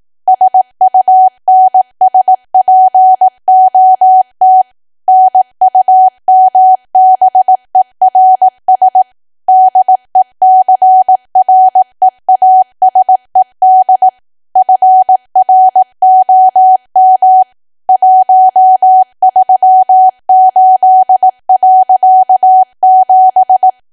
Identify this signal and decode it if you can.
Morse code, heard as a keyed tone, about 18 wpm, 750 Hz. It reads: SUNSPOT NUMBERS DECREASED FROM 138.7
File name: cwa / W1AW-Level2 / 18WPM